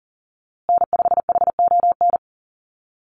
Morse code, heard as a keyed tone, about 38 wpm, 700 Hz. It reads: N5HON
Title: USA Call Signs Repeated 2x 40wpm